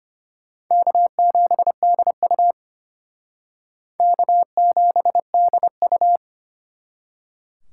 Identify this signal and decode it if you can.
Morse code, heard as a keyed tone, about 29 wpm, 700 Hz. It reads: K7DU K7DU
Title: Speed Racing - USA Call Signs 25wpm